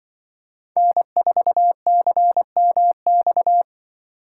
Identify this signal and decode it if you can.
Morse code, heard as a keyed tone, about 24 wpm, 700 Hz. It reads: N4CMX
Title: Speed Racing - USA 1x3 Call Signs 20wpm